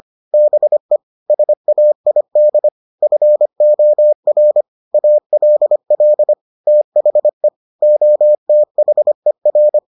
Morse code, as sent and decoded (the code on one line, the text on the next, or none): -... / ... .- .. -.. / ..-. --- .-. / .- .-.. .-.. / - .... / --- - .... .-.
BE SAID FOR ALL THE OTHER